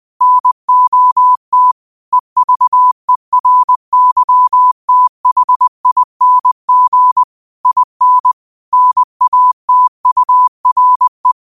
Morse code, heard as a keyed tone, about 20 wpm, 1000 Hz. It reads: NOT EVERYTHING IN NATURE
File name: qotdfiles / 20wpm